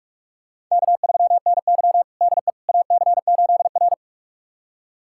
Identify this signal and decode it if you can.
Morse code, heard as a keyed tone, about 46 wpm, 700 Hz. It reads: K3NY DE AC8R
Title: USA Callsign DE Callsign - 4 Characters - 45wpm